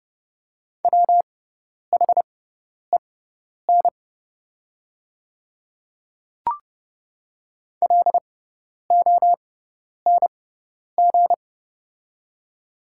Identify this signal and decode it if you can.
Morse code, heard as a keyed tone, about 30 wpm, spaced as for 10 wpm, 690 Hz. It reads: WHEN LONG
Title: 6x Character Spacing - Top 100 Words - 30wpm